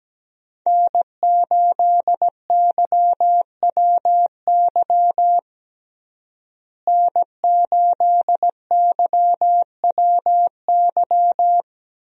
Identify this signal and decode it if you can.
Morse code, heard as a keyed tone, about 17 wpm, 700 Hz. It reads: N8YWY N8YWY
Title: USA Call Signs - 5 Characters - Repeated 3x - 17wpm